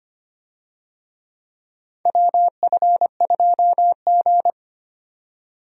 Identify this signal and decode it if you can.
Morse code, heard as a keyed tone, about 25 wpm, 700 Hz. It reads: WF2G
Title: USA Call Signs - 4 Characters - 25wpm